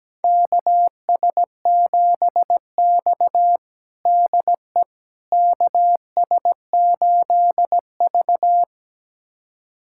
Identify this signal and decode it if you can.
Morse code, heard as a keyed tone, about 17 wpm, 700 Hz. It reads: KS7X DE KS8V
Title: USA Callsign DE Callsign - 4 Characters - 17wpm